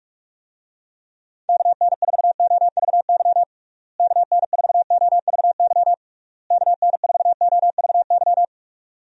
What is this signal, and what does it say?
Morse code, heard as a keyed tone, about 46 wpm, 700 Hz. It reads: KN4OVY KN4OVY KN4OVY
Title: USA Call Signs - 6 Characters - Repeated 3x - 45wpm